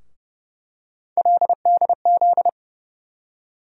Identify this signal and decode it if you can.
Morse code, heard as a keyed tone, about 30 wpm, 700 Hz. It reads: LDZ
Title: USA Call Sign Postfixes - 3 Letters 30wpm